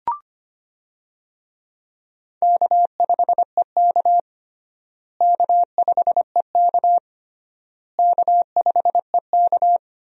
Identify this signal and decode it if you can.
Morse code, heard as a keyed tone, about 25 wpm, 700 Hz. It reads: K5EK K5EK K5EK
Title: USA Call Signs - 4 Characters - Repeated 3x - 25wpm